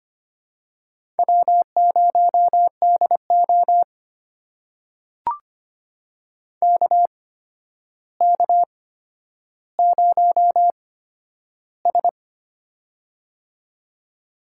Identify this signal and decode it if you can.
Morse code, heard as a keyed tone, about 25 wpm, spaced as for 11 wpm, 700 Hz. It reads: W0DO KK0S
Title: USA Call Signs - 4 Characters - 8x Character Spacing - 25wpm